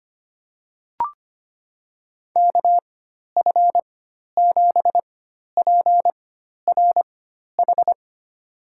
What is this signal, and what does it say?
Morse code, heard as a keyed tone, about 25 wpm, spaced as for 12 wpm, 700 Hz. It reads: KF7PRH
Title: USA Call Signs - 4x Character Spacing - 25wpm